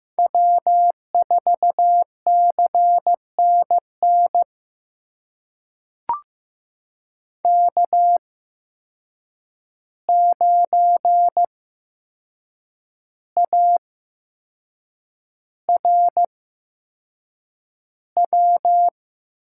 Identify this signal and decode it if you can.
Morse code, heard as a keyed tone, about 15 wpm, spaced as for 6 wpm, 700 Hz. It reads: W4CNN K9ARW
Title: USA Call Signs - 5 Characters - 8x Character Spacing - 15wpm